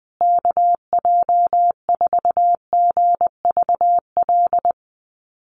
Morse code, heard as a keyed tone, about 19 wpm, 700 Hz. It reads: KJ4GVL
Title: USA Call Signs Repeated 3x 20wpm